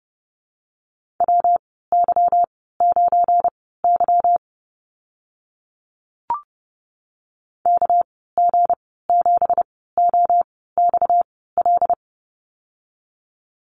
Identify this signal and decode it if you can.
Morse code, heard as a keyed tone, about 32 wpm, spaced as for 17 wpm, 700 Hz. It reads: WY9Y KG7OXL
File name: Lesson 37g - Letter x - Call Signs 30wpm